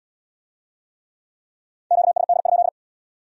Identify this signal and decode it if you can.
Morse code, heard as a keyed tone, about 69 wpm, 700 Hz. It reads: QSD?